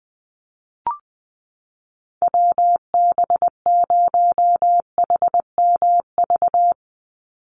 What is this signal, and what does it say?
Morse code, heard as a keyed tone, about 20 wpm, 700 Hz. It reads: WB0HMV
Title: USA Call Signs 20wpm